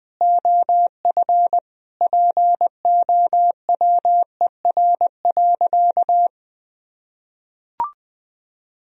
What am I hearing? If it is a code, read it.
Morse code, heard as a keyed tone, about 20 wpm, 700 Hz. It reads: OF POWER.
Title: Sentences from Top 1000 Words - Review Part 4 - 20wpm